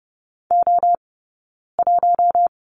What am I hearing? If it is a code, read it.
Morse code, heard as a keyed tone, about 30 wpm, spaced as for 9 wpm, 700 Hz. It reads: O1
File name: Lesson 18d - Letter m - Two Characters 30wpm